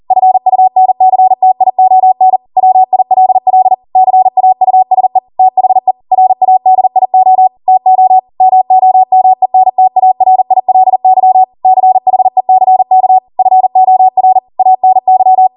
Morse code, heard as a keyed tone, about 40 wpm, 750 Hz. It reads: FUNCTION WILL CAUSE THE RADIO TO MOMENTARILY CHECK FOR ANY
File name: Morse / 40WPM